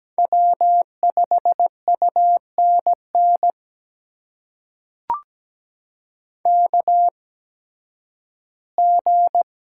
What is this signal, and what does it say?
Morse code, heard as a keyed tone, about 17 wpm, spaced as for 11 wpm, 700 Hz. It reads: W5UNN KG
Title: USA Call Signs - 8x Character Spacing - 17wpm